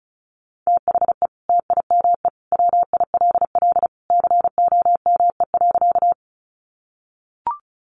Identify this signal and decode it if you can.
Morse code, heard as a keyed tone, about 34 wpm, 700 Hz. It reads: THE TIME WILL COME.
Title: Sentences from Top 100 Words 35wpm